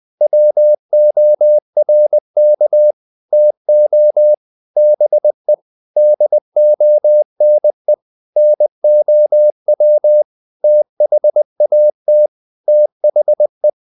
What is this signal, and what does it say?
Morse code, heard as a keyed tone, about 20 wpm, 600 Hz. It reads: WORK TO BE DONE NOW THAT THE
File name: the old man and the sea part 2 20wpm